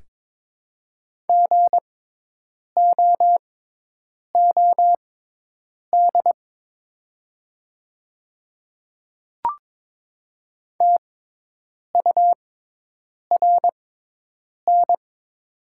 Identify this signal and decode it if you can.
Morse code, heard as a keyed tone, about 22 wpm, spaced as for 8 wpm, 700 Hz. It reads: GOOD TURN
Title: 6x Character Spacing - Top 200 Words - 22wpm